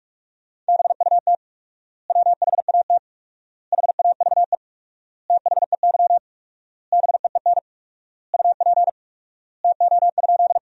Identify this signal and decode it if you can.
Morse code, heard as a keyed tone, about 47 wpm, 700 Hz. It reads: BUT WHAT HAVE THEY BEEN UP TO?
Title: 4x Word-Spacing - Sentences from Top 100 Words - 45wpm